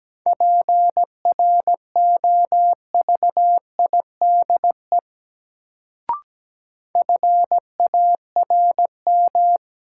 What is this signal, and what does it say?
Morse code, heard as a keyed tone, about 17 wpm, 700 Hz. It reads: PROVIDE FARM
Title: Sentences from Top 1000 Words - Review Part 2 - New Word Drill - 17wpm